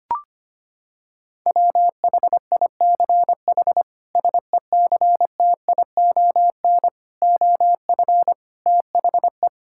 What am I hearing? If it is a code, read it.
Morse code, heard as a keyed tone, about 25 wpm, 700 Hz. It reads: WHICH SECTION OF THE
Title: Sentences from Top 1000 Words - Review Part 5 - 25wpm